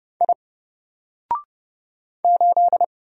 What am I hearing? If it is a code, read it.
Morse code, heard as a keyed tone, about 30 wpm, 700 Hz. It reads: I 8
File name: Single-Letter-Number-30wpm